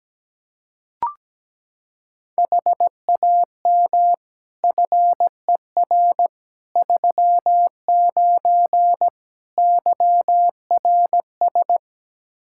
Morse code, heard as a keyed tone, about 17 wpm, 700 Hz. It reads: HAM FER 39 YRS
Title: Practice QSO - Exchange 3 - More Info - 17wpm